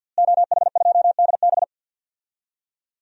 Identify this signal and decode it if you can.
Morse code, heard as a keyed tone, about 52 wpm, 700 Hz. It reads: OH2DB
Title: DXCC-100 Call Signs - 5 Characters - Repeated 2x - 50wpm